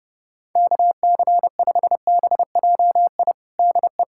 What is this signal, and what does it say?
Morse code, heard as a keyed tone, about 30 wpm, 700 Hz. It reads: KC5BJI DE
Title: USA Callsign DE Callsign - 6 Characters - 30wpm